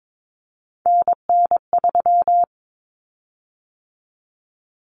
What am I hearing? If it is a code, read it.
Morse code, heard as a keyed tone, about 22 wpm, 700 Hz. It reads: NN3
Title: Common Cut Numbers - 3-Digit 22wpm